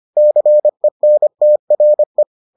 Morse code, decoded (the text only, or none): CENTRE